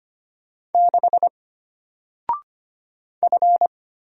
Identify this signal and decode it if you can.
Morse code, heard as a keyed tone, about 25 wpm, 700 Hz. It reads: 6 F
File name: Single-Letter-Number-25wpm